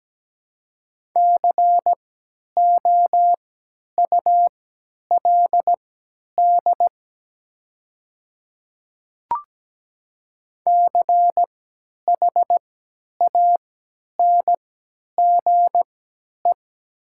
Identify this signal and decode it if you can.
Morse code, heard as a keyed tone, about 17 wpm, spaced as for 10 wpm, 700 Hz. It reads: COULD CHANGE
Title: 3x Character Spacing - Top 500 Words - 17wpm